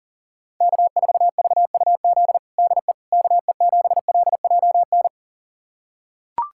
Morse code, heard as a keyed tone, about 40 wpm, 700 Hz. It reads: K4VUZ DE KE7LJN
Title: USA Callsign DE Callsign 40wpm